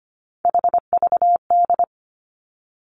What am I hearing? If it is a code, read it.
Morse code, heard as a keyed tone, about 25 wpm, 700 Hz. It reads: HVD
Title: Three Challenging Letter-Number Combinations 25wpm